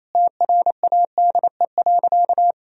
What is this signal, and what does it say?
Morse code, heard as a keyed tone, about 28 wpm, 700 Hz. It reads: TRADE.
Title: Sentences from Top 1000 Words - Review Part 4 - 28wpm